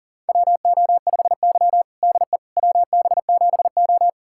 Morse code, heard as a keyed tone, about 40 wpm, 700 Hz. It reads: WO5Y DE WB7O
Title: USA Callsign DE Callsign - 4 Characters - 40wpm